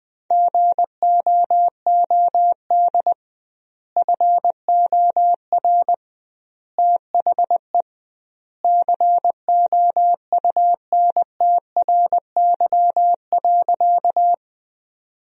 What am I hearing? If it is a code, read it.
Morse code, heard as a keyed tone, about 21 wpm, 700 Hz. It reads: GOOD FOR THE COUNTRY.
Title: 2x Word-Spacing - Sentences from Top 300 Words - 20wpm